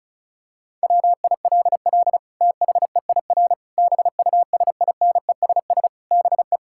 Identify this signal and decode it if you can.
Morse code, heard as a keyed tone, about 36 wpm, 700 Hz. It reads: WILL THEIR BUSINESS BE
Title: Sentences from Top 1000 Words - Review Part 5 - 35wpm